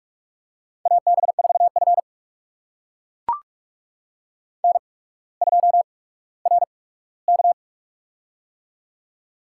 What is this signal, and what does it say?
Morse code, heard as a keyed tone, about 44 wpm, spaced as for 19 wpm, 700 Hz. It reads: AB4F N2RK
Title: USA Call Signs - 4 Characters - 8x Character Spacing - 45wpm